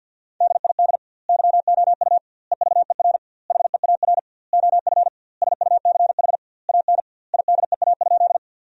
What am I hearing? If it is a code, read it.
Morse code, heard as a keyed tone, about 50 wpm, 700 Hz. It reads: DID YOU EVER HEAR OF SUCH AN IDEA?
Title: 2x Word-Spacing - Sentences from Top 400 Words - 50wpm